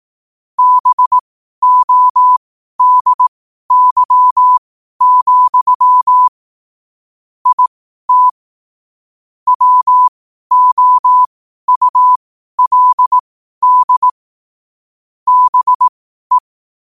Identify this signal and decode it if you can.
Morse code, heard as a keyed tone, about 18 wpm, spaced as for 12 wpm, 1000 Hz. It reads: BODY, IT WOULD BE
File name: qotdfiles / 12wpm